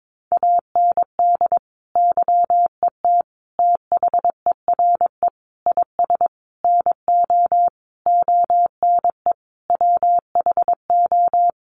Morse code, heard as a keyed tone, about 22 wpm, 700 Hz. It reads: AND YET THERE IS NO ONE WHO
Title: Sentences from Top 1000 Words 22wpm